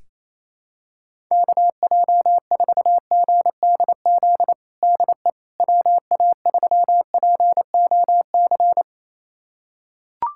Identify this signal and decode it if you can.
Morse code, heard as a keyed tone, about 28 wpm, 700 Hz. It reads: KJ4GDZ DE WA3POC
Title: USA Callsign DE Callsign 28wpm